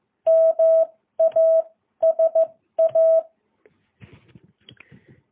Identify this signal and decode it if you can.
Morse code, heard as a keyed tone, about 15 wpm, spaced as for 12 wpm, 650 Hz. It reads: MASA
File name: jh8cbh